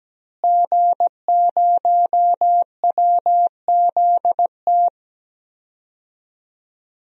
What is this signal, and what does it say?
Morse code, heard as a keyed tone, about 17 wpm, 700 Hz. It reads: G0WZT
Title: DXCC-100 Call Signs - 5 Characters - Repeated 2x - 17wpm